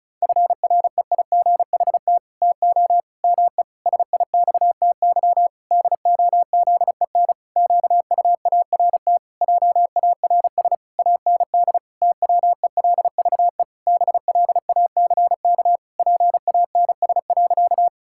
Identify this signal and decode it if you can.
Morse code, heard as a keyed tone, about 36 wpm, 700 Hz. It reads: FREIGHT TO ME SIXTY DOZEN QUART JARS AND TWELVE BLACK PANS.
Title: Pangram Challenge v2 - 35wpm